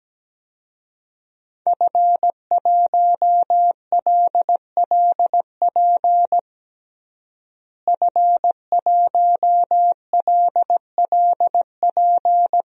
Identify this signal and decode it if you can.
Morse code, heard as a keyed tone, about 17 wpm, 700 Hz. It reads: F1LLP F1LLP
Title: DXCC-100 Call Signs - 5 Characters - Repeated 2x - 17wpm